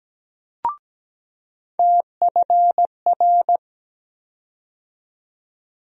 Morse code, heard as a keyed tone, about 17 wpm, 700 Hz. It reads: TFR
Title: USA Call Sign Postfixes - 3 Letters 17wpm